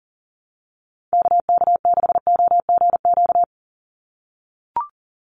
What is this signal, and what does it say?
Morse code, heard as a keyed tone, about 40 wpm, 700 Hz. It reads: KK6OGQ
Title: USA Call Signs 40wpm